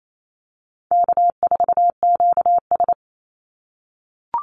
Morse code, heard as a keyed tone, about 28 wpm, 700 Hz. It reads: K4QS
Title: NA SOTA Activator Callsigns 28wpm